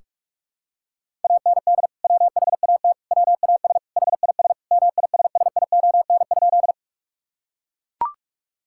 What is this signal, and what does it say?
Morse code, heard as a keyed tone, about 43 wpm, 700 Hz. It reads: AND WHAT WAS HIS MISSION?
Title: Sentences from Top 1600 Words 45wpm